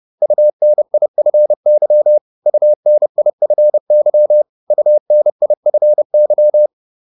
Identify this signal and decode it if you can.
Morse code, heard as a keyed tone, about 30 wpm, 600 Hz. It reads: UNIFY UNIFY UNIFY